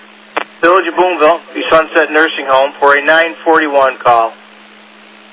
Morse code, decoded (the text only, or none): none